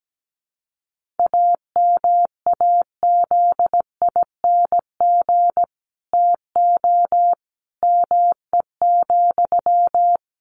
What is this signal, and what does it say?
Morse code, heard as a keyed tone, about 17 wpm, 700 Hz. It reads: AMAZING TO ME,
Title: Sentences from Top 1500 Words 17wpm